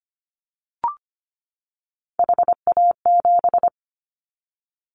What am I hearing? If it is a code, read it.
Morse code, heard as a keyed tone, about 25 wpm, 700 Hz. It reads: HA7
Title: Callsign Prefixes for Top 25 DXCC - 25wpm